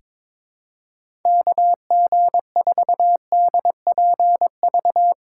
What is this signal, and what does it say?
Morse code, heard as a keyed tone, about 22 wpm, 700 Hz. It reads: KG4DPV
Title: USA Call Signs - 6 Characters - 6x Character Spacing - 22wpm